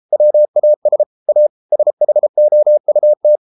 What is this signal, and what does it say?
Morse code, heard as a keyed tone, about 33 wpm, 600 Hz. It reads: WAS A SHOUT